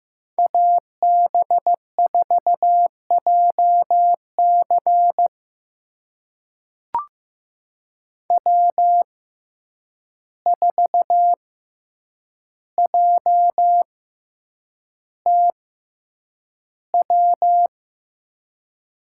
Morse code, heard as a keyed tone, about 15 wpm, spaced as for 8 wpm, 700 Hz. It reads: AB4JC W4JTW
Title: USA Call Signs - 5 Characters - 6x Character Spacing - 15wpm